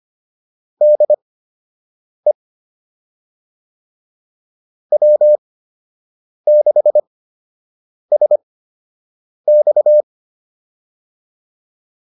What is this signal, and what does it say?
Morse code, heard as a keyed tone, about 25 wpm, spaced as for 7 wpm, 600 Hz. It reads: DE W6SX